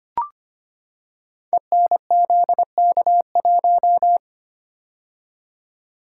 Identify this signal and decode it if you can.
Morse code, heard as a keyed tone, about 25 wpm, 700 Hz. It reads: ENZK1
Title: Code Groups - 5 Letter:Numbers - 25wpm